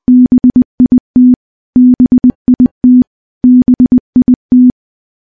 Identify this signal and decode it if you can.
Morse code, heard as a keyed tone, about 20 wpm, 250 Hz. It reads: BIT BIT BIT
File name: morse-training-20WPM